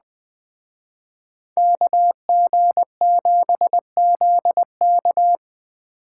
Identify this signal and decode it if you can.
Morse code, heard as a keyed tone, about 20 wpm, 700 Hz. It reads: KG7ZK
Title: Speed Racing - USA 2x2 Call Signs 17wpm